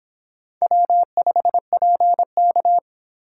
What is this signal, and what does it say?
Morse code, heard as a keyed tone, about 25 wpm, 700 Hz. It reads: W5PK